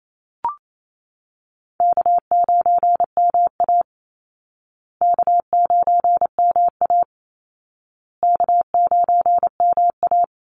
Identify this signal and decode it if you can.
Morse code, heard as a keyed tone, about 28 wpm, 700 Hz. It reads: K9MA K9MA K9MA
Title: Easy Call Signs Repeated 3x - 28wpm